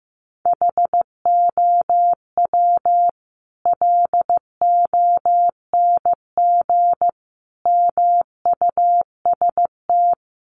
Morse code, heard as a keyed tone, about 15 wpm, 700 Hz. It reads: HOW LONG MUST
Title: Sentences from Top 1200 Words 15wpm